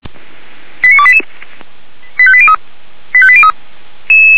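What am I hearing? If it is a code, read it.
Five-tone selective call (ZVEI), heard as quick runs of five tones, tones of about 70 ms. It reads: 89288 85803 85803
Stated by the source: ZVEI five-tone call received by SDR, fire brigade Orlamünde (feuerwehr-orlamuende).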